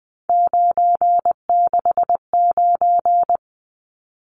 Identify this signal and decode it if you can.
Morse code, heard as a keyed tone, about 20 wpm, 700 Hz. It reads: 969
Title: Numbers - 3-Digit 20wpm